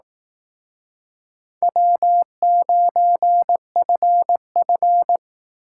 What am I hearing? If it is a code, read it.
Morse code, heard as a keyed tone, about 18 wpm, 700 Hz. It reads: W9FF
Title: Speed Racing - USA Call Signs 15wpm